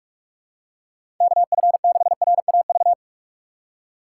Morse code, heard as a keyed tone, about 46 wpm, 700 Hz. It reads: KF6RAV
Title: USA Call Signs - 6 Characters - 3x Character Spacing - 45wpm